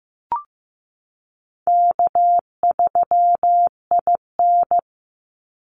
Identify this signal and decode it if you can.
Morse code, heard as a keyed tone, about 15 wpm, 700 Hz. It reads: K3IN
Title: Easy Call Signs - 15wpm